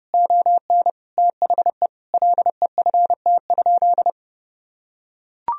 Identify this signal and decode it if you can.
Morse code, heard as a keyed tone, about 30 wpm, 700 Hz. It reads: ON THE LEFT?
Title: Sentences from Top 1000 Words - Review Part 2 - 30wpm